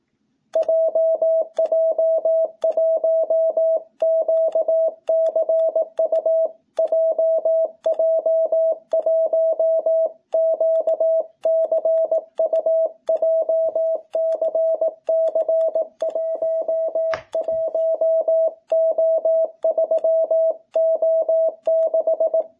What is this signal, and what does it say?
Morse code, heard as a keyed tone, about 18 wpm, 660 Hz. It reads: JJ1QCU JJ1QCU JCC 11O3O6
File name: jh8cbh / 2021cw